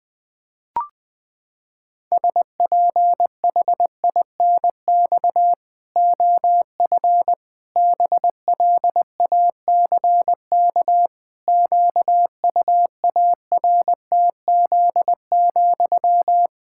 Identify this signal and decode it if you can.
Morse code, heard as a keyed tone, about 21 wpm, 700 Hz. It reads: SPHINX OF BLACK QUARTZ,
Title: Pangram Challenge v2 - 20wpm